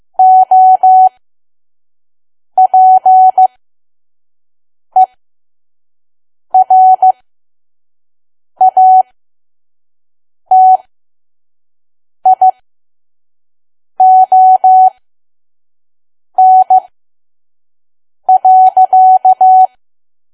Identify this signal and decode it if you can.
Morse code, heard as a keyed tone, about 15 wpm, spaced as for 5 wpm, 750 Hz. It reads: OPERATION.
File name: Archive / 5 WPM files / PM